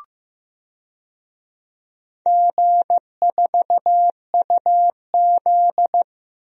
Morse code, heard as a keyed tone, about 15 wpm, 700 Hz. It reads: G4UZ
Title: DXCC-100 Call Signs - 4 Characters - Repeated 2x - 15wpm